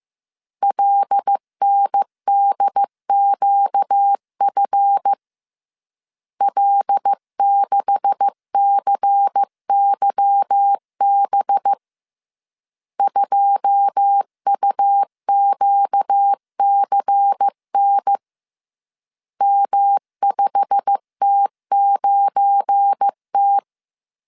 Morse code, heard as a keyed tone, about 15 wpm, 800 Hz. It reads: LNDQF L6CYB 2UQCN M5T9T
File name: CW / DET / 1 9 1500C S C 14wpm 3245